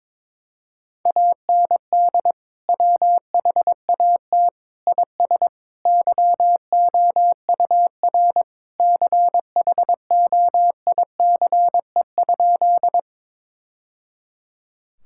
Morse code, heard as a keyed tone, about 22 wpm, 700 Hz. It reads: AND WHAT IS YOUR CHOICE?
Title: Sentences from Top 1000 Words - Review Part 6 - 22wpm